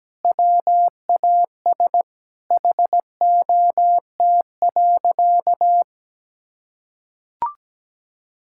Morse code, heard as a keyed tone, about 17 wpm, 700 Hz. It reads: WAS HOT.
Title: Sentences from Top 1000 Words - Review Part 5 - 17wpm